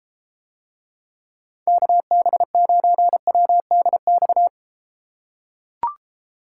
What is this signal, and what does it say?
Morse code, heard as a keyed tone, about 33 wpm, 700 Hz. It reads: KB9WDX